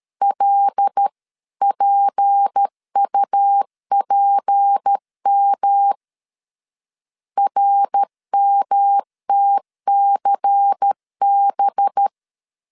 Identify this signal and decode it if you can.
Morse code, heard as a keyed tone, about 13 wpm, 800 Hz. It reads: LPUPM RMTCB
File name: LEZIONI / 6 PKX0 / 6 P 12wpm 2033